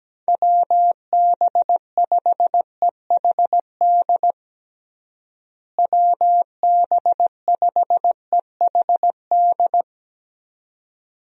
Morse code, heard as a keyed tone, about 17 wpm, 700 Hz. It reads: WB5EHD WB5EHD
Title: USA Call Signs - 6 Characters - Repeated 2x - 17wpm